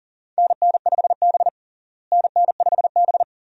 Morse code, heard as a keyed tone, about 37 wpm, 700 Hz. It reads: NN5B NN5B
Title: USA Call Signs - 4 Characters - Repeated 2x - 40wpm